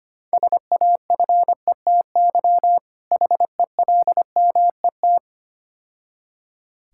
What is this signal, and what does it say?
Morse code, heard as a keyed tone, about 28 wpm, 700 Hz. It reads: SAFETY HELMET
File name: Difficult Words - Speed-Racing with Hints - 25wpm